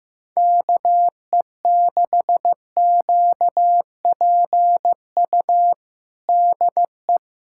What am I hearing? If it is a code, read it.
Morse code, heard as a keyed tone, about 15 wpm, 700 Hz. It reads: KE6QPU DE
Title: USA Callsign DE Callsign 15wpm